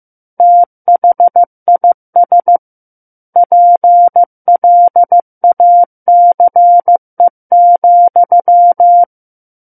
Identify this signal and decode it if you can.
Morse code, heard as a keyed tone, about 15 wpm, 700 Hz. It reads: THIS PLACE,
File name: Book One / 06-Chapter 6 - 15 WPM